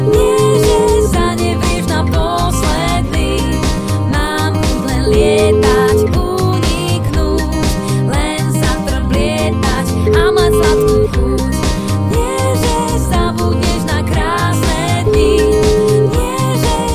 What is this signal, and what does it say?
Ringback tone (Europe, 425 Hz), with noise.